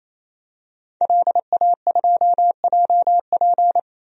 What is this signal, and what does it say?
Morse code, heard as a keyed tone, about 28 wpm, 700 Hz. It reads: LA2JP
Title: DXCC-100 Call Signs - 5 Characters - 28wpm